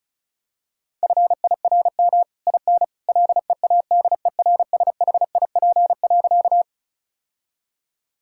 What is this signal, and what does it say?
Morse code, heard as a keyed tone, about 36 wpm, 700 Hz. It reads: FIRM IN LEADERSHIP.
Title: Sentences from Top 1600 Words 35wpm